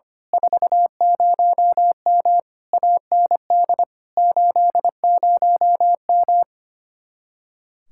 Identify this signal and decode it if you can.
Morse code, heard as a keyed tone, about 24 wpm, 700 Hz. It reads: 40M AND 80M
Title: Speed-Racing Common QSO Phrases - 25wpm